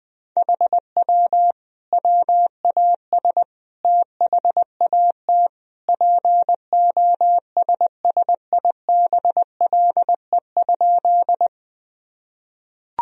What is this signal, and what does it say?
Morse code, heard as a keyed tone, about 21 wpm, 700 Hz. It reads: HW WAS THAT POSSIBLE?
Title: Sentences from Top 500 Words and Abbreviations 20wpm